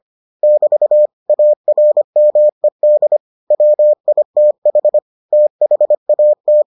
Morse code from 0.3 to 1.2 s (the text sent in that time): <BT>